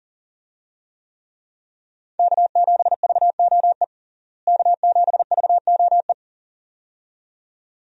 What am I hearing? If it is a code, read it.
Morse code, heard as a keyed tone, about 40 wpm, 700 Hz. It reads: K7VOE K7VOE
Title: USA Call Signs - 5 Characters - Repeated 2x - 40wpm